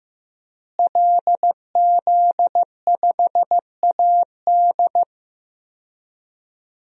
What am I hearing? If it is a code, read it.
Morse code, heard as a keyed tone, about 15 wpm, 700 Hz. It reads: LZ5AD
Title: DXCC-100 Call Signs - 5 Characters - Repeated 2x - 15wpm